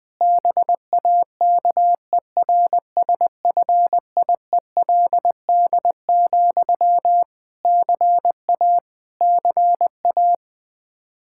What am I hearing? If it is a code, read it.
Morse code, heard as a keyed tone, about 20 wpm, 700 Hz. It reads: BAKERSFIELD, CA CA
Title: Practice QSO - Exchange 1 - Beginning - 20wpm